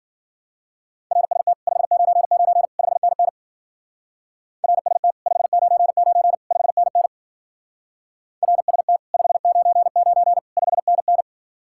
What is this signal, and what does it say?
Morse code, heard as a keyed tone, about 51 wpm, 700 Hz. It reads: RST 599 5NN RST 599 5NN RST 599 5NN